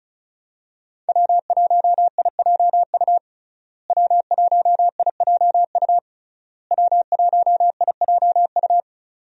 Morse code, heard as a keyed tone, about 36 wpm, 700 Hz. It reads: W1IJU W1IJU W1IJU
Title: USA Call Signs - 5 Characters - Repeated 3x - 35wpm